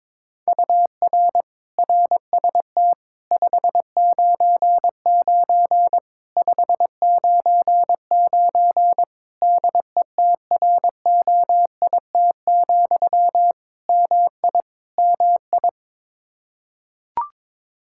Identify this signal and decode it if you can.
Morse code, heard as a keyed tone, about 22 wpm, 700 Hz. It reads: UR RST 599 599 DETROIT, MI MI
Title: Practice QSO - Exchange 1 - Beginning - 22wpm